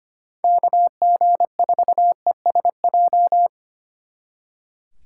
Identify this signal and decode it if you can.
Morse code, heard as a keyed tone, about 29 wpm, 700 Hz. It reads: KG4ESJ
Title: Speed Racing - USA Call Signs 25wpm